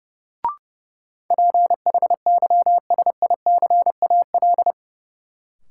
Morse code, heard as a keyed tone, about 30 wpm, 700 Hz. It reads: PHYSICAL
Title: Sentences from Top 1000 Words - Review Part 4 - New Word Drill - 30wpm